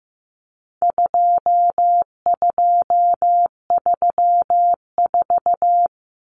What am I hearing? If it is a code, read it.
Morse code, heard as a keyed tone, about 15 wpm, 700 Hz. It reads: 2234
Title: Common Cut Numbers - 4-Digit 15wpm